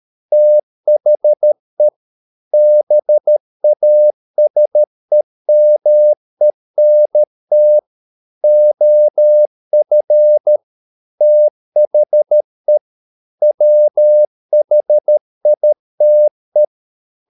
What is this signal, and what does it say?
Morse code, heard as a keyed tone, about 13 wpm, 600 Hz. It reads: THE BASEMENT OF THE WHITE